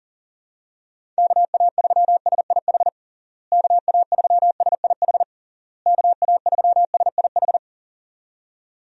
Morse code, heard as a keyed tone, about 40 wpm, 700 Hz. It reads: KA3SIH KA3SIH KA3SIH
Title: USA Call Signs - 6 Characters - Repeated 3x - 40wpm